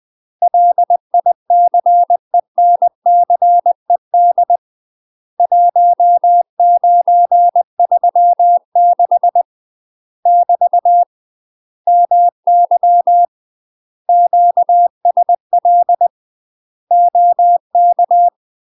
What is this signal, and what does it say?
Morse code, heard as a keyed tone, about 20 wpm, 700 Hz. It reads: LICENCED 1936 <BT> MY QSL OK
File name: UK MASTER QSOs TRAINING / 20WPM / UK MASTER QSOs 0036